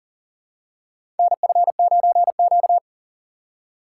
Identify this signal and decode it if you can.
Morse code, heard as a keyed tone, about 40 wpm, 700 Hz. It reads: NF9Q
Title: USA Call Signs - 4 Characters - 2x Character Spacing - 40wpm